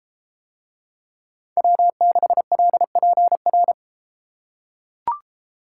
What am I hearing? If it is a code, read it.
Morse code, heard as a keyed tone, about 33 wpm, 700 Hz. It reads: W6LPR